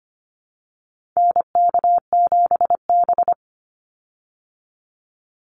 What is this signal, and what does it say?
Morse code, heard as a keyed tone, about 25 wpm, 700 Hz. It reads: NK7B
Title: Call Signs Repeated 3x - 25wpm